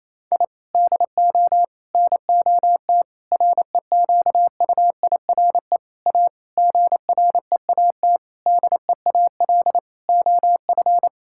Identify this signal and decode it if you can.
Morse code, heard as a keyed tone, about 28 wpm, 700 Hz. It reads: I DO NOT REQUIRE A GREAT DEAL OF
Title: Sentences from Top 1000 Words - Review Part 6 - 28wpm